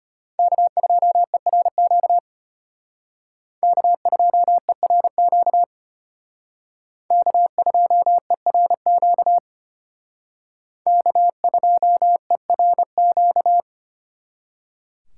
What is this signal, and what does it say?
Morse code, heard as a keyed tone, about 33 wpm, 700 Hz. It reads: K2ERQ K2ERQ K2ERQ K2ERQ